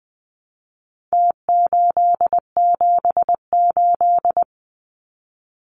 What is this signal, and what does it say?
Morse code, heard as a keyed tone, about 20 wpm, 700 Hz. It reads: T878
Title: Numbers - Contest Style - 4-Digit 20wpm